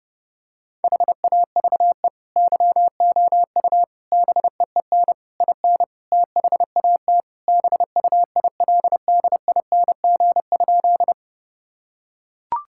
Morse code, heard as a keyed tone, about 30 wpm, 700 Hz. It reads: HAVE YOU BEEN IN THAT BUILDING?
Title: Sentences from Top 1000 Words - Review Part 5 - 30wpm